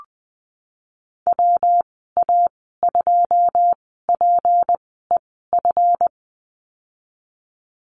Morse code, heard as a keyed tone, about 20 wpm, spaced as for 15 wpm, 700 Hz. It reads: WA2PEF
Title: 161 Lesson 26g - Letter g - Call Signs 20wpm